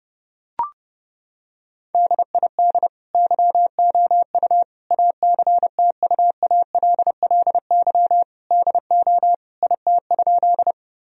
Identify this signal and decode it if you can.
Morse code, heard as a keyed tone, about 30 wpm, 700 Hz. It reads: DID YOU ACTUALLY DO IT?
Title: Sentences from Top 1000 Words - Review Part 4 - 30wpm